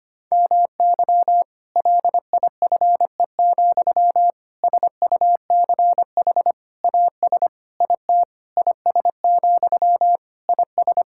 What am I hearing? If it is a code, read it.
Morse code, heard as a keyed tone, about 25 wpm, 700 Hz. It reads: MY LIFE, SUCH AS IT IS, IS
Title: Sentences from Top 1600 Words 25wpm